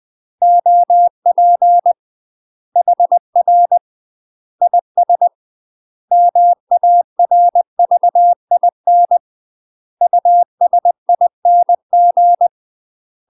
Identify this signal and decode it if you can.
Morse code, heard as a keyed tone, about 20 wpm, 700 Hz. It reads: OP HR IS MARVIN USING
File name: UK MASTER QSOs TRAINING / 20WPM / UK MASTER QSOs 0024